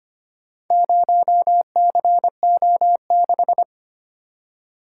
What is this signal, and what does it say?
Morse code, heard as a keyed tone, about 25 wpm, 700 Hz. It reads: 0CO6
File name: practice / 25wpm / Four Letter-Number Combinations - 25wpm